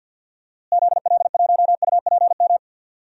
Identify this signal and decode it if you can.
Morse code, heard as a keyed tone, about 52 wpm, 700 Hz. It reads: ZL1UPM
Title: DXCC-100 Call Signs - 6 Characters - Repeated 3x - 50wpm